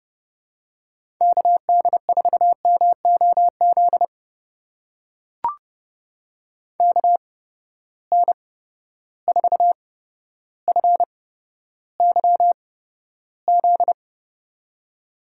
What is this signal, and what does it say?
Morse code, heard as a keyed tone, about 30 wpm, spaced as for 13 wpm, 700 Hz. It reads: KD4MOZ KN4FYZ
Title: USA Call Signs - 6 Characters - 8x Character Spacing - 30wpm